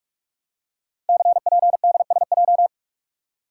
Morse code, heard as a keyed tone, about 44 wpm, 700 Hz. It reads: CPDSJ
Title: Code Groups - 5 Letter:Numbers - 45wpm